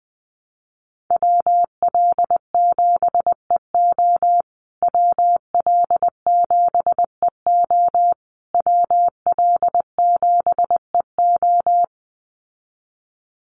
Morse code, heard as a keyed tone, about 20 wpm, 700 Hz. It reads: WL7EO WL7EO WL7EO
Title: USA Call Signs Repeated 3x 20wpm